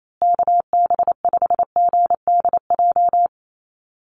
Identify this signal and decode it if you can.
Morse code, heard as a keyed tone, about 28 wpm, 700 Hz. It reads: KB5GDJ
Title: USA Call Signs Repeated 2x 28wpm